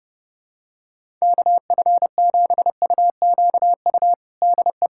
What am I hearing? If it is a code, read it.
Morse code, heard as a keyed tone, about 32 wpm, 700 Hz. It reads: KF7UQU DE
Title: USA Callsign DE Callsign - 6 Characters - 30wpm